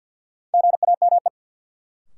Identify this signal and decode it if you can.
Morse code, heard as a keyed tone, about 48 wpm, 700 Hz. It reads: GAME
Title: Sentences from Top 500 Words - New Word Drill - 50wpm